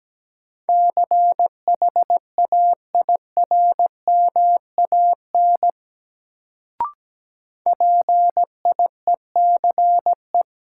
Morse code, heard as a keyed tone, about 17 wpm, 700 Hz. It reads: CHAIRMAN PIECE
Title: Sets of 3 Words - New Word Drill - 17wpm